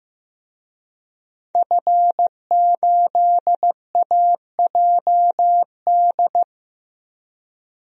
Morse code, heard as a keyed tone, about 15 wpm, 700 Hz. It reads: F8AJD
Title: DXCC-100 Call Signs - 5 Characters - 15wpm